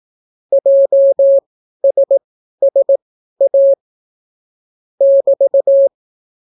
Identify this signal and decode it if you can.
Morse code, heard as a keyed tone, about 18 wpm, spaced as for 12 wpm, 550 Hz. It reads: JSSA <BT>